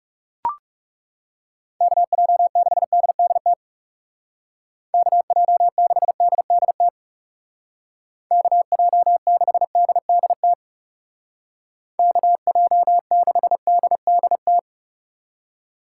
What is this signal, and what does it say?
Morse code, heard as a keyed tone, about 36 wpm, 700 Hz. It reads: KJ6DDT KJ6DDT KJ6DDT KJ6DDT
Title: Speed Racing - USA 2x3 Call Signs 30wpm